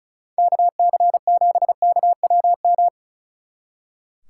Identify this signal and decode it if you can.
Morse code, heard as a keyed tone, about 36 wpm, 700 Hz. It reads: KC7KWM
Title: Speed Racing - USA 2x3 Call Signs 35wpm